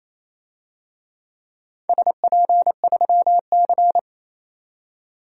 Morse code, heard as a keyed tone, about 28 wpm, 700 Hz. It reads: SP3C
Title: DXCC-100 Call Signs - 4 Characters - 28wpm